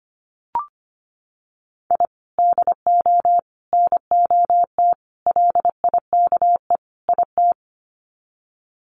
Morse code, heard as a keyed tone, about 25 wpm, 700 Hz. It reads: I DO NOT LIKE IT
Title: Sets of 5 Words Encore 25wpm